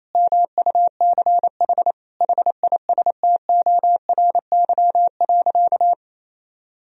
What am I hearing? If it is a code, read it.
Morse code, heard as a keyed tone, about 28 wpm, 700 Hz. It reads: MUCH HISTORY.
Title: Sentences from Top 1000 Words - Review Part 2 - 28wpm